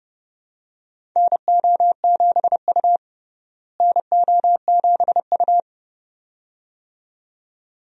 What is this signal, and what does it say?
Morse code, heard as a keyed tone, about 31 wpm, 700 Hz. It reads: NO7U NO7U
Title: USA Call Signs - 4 Characters - Repeated 2x - 30wpm